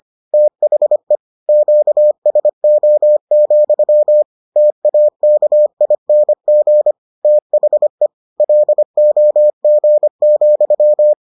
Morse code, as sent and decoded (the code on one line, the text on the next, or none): - .... / --.- ... --- --..-- / - .- -.- .. -. --. / - .... / .-.. --- --. --..--
THE QSO, TAKING THE LOG,